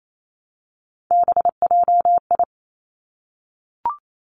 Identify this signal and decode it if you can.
Morse code, heard as a keyed tone, about 28 wpm, 700 Hz. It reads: BJI
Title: Three Challenging Letter-Number Combinations 28wpm